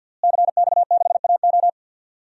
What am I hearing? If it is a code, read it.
Morse code, heard as a keyed tone, about 48 wpm, 700 Hz. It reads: CX6AO
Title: DXCC-100 Call Signs - 5 Characters - Repeated 3x - 50wpm